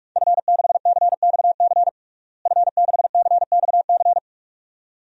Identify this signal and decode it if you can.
Morse code, heard as a keyed tone, about 47 wpm, 700 Hz. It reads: F6CXC F6CXC
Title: DXCC-100 Call Signs - 5 Characters - Repeated 2x - 45wpm